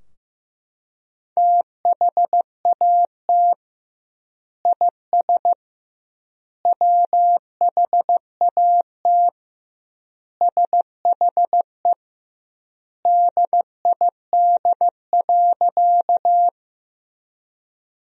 Morse code, heard as a keyed tone, about 16 wpm, 700 Hz. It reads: THAT IS WHAT SHE DID.
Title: 2x Word-Spacing - Sentences from Top 100 Words - 15wpm